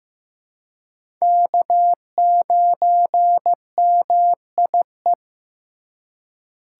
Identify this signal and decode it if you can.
Morse code, heard as a keyed tone, about 15 wpm, 700 Hz. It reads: K9MIE
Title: USA Call Signs - 4x Character Spacing - 15wpm